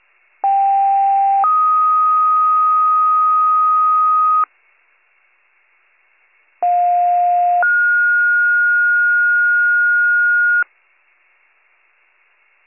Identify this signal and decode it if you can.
Two-tone page, heard as a one-second tone, then a three-second tone, about 790 then 1250 Hz, about 710 then 1430 Hz. It is heard from 0.4 to 10.6 s.